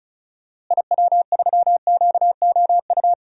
Morse code, heard as a keyed tone, about 34 wpm, 700 Hz. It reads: IW3QOU
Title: DXCC-100 Call Signs - 6 Characters - Repeated 3x - 35wpm